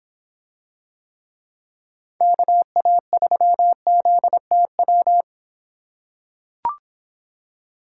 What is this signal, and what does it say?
Morse code, heard as a keyed tone, about 24 wpm, 700 Hz. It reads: KA3ZTW